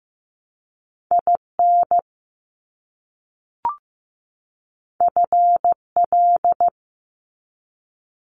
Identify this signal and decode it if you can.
Morse code, heard as a keyed tone, about 15 wpm, 700 Hz. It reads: IN FL
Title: US States 15wpm